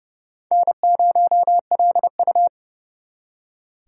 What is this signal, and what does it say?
Morse code, heard as a keyed tone, about 40 wpm, 700 Hz. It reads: N0LU